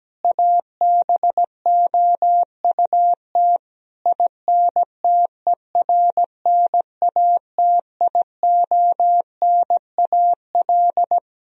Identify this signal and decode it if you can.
Morse code, heard as a keyed tone, about 17 wpm, 700 Hz. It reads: ABOUT INTERNATIONAL
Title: Sentences from Top 1000 Words - Review Part 4 - 17wpm